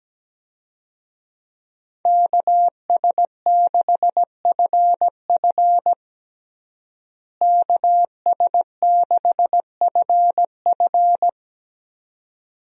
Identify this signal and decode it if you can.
Morse code, heard as a keyed tone, about 17 wpm, 700 Hz. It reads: KS6FF KS6FF
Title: USA Call Signs - 5 Characters - Repeated 2x - 17wpm